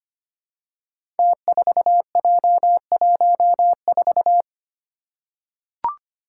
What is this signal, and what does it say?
Morse code, heard as a keyed tone, about 25 wpm, 700 Hz. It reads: T4J14
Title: Five Challenging Letter-Number Combinations 25wpm